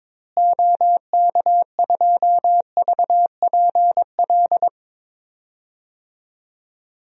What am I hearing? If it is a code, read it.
Morse code, heard as a keyed tone, about 21 wpm, 700 Hz. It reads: OK2VPL
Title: DXCC-100 Call Signs - 6 Characters - Repeated 2x - 22wpm